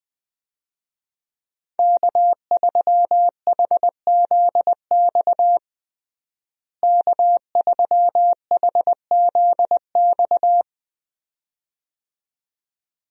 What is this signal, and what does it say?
Morse code, heard as a keyed tone, about 21 wpm, 700 Hz. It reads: K3HZX K3HZX
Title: USA Call Signs - 5 Characters - Repeated 2x - 20wpm